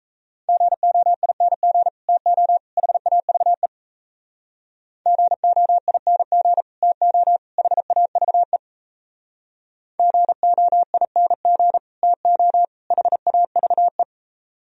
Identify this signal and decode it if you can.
Morse code, heard as a keyed tone, about 34 wpm, 700 Hz. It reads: GOING TO HAVE GOING TO HAVE GOING TO HAVE
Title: Speed-Racing - Sets of 3 Words 28wpm